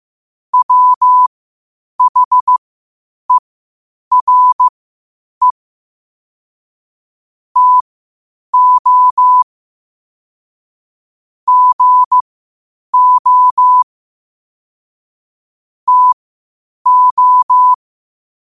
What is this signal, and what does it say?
Morse code, heard as a keyed tone, about 15 wpm, spaced as for 8 wpm, 1000 Hz. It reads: WHERE TO GO TO
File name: qotdfiles / 8wpm